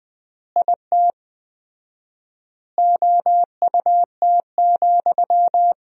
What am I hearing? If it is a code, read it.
Morse code, heard as a keyed tone, about 21 wpm, 700 Hz. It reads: IT OUT,
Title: 4x Word-Spacing - Sentences from Top 300 Words - 20wpm